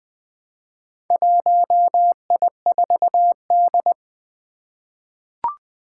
Morse code, heard as a keyed tone, about 20 wpm, 700 Hz. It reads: 1I4D
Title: Four Challenging Letter-Number Combinations 20wpm